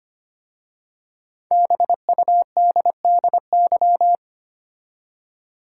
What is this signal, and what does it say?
Morse code, heard as a keyed tone, about 25 wpm, 700 Hz. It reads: BUDDY
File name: 5 Letter Words - First 3 Letters Given - 25wpm